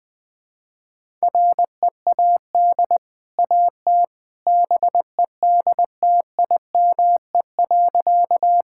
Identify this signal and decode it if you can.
Morse code, heard as a keyed tone, about 20 wpm, 700 Hz. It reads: READ AT BEDTIME.
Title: Easy Sentences with Hints - 20wpm